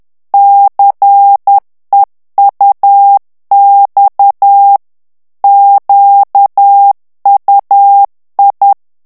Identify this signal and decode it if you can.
Morse code, heard as a keyed tone, about 10 wpm, 800 Hz. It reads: CEUX QUI
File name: CW / FICHIERS-MP3 / FRANCAIS TEXTE CLAIR 2 VITESSE 10 MOTS MINUTES